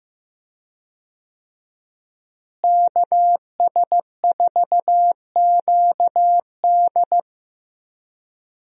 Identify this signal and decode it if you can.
Morse code, heard as a keyed tone, about 15 wpm, 700 Hz. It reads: KS4QD
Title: USA Call Signs - 5 Characters - Repeated 2x - 15wpm